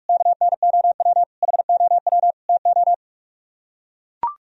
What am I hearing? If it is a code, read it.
Morse code, heard as a keyed tone, about 44 wpm, 700 Hz. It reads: KNOW HOW TO